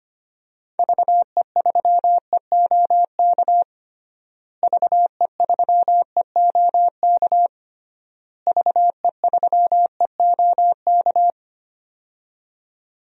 Morse code, heard as a keyed tone, about 25 wpm, 700 Hz. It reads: VE3EOK VE3EOK VE3EOK
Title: DXCC-100 Call Signs - 6 Characters - Repeated 3x - 25wpm